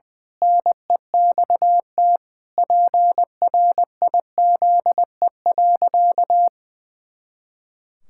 Morse code, heard as a keyed tone, about 19 wpm, 700 Hz. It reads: NEXT PRIZE.
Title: Pangram Challenge v2 - 20wpm